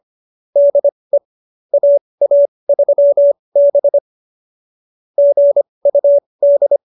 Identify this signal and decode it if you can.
Morse code, heard as a keyed tone, about 25 wpm, spaced as for 20 wpm, 570 Hz. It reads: DE AA3B GUD
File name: QSO302 20